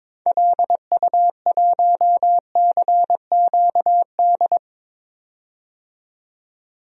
Morse code, heard as a keyed tone, about 22 wpm, 700 Hz. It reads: LU1CQD
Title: DXCC-100 Call Signs - 6 Characters - 22wpm